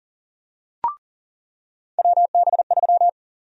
Morse code, heard as a keyed tone, about 40 wpm, 700 Hz. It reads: WB3
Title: USA Call Sign Prefixes - 40wpm